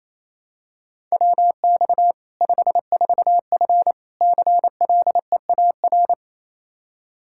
Morse code, heard as a keyed tone, about 28 wpm, 700 Hz. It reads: WX 54F CLEAR
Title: K1USN SST QSO Practice 28wpm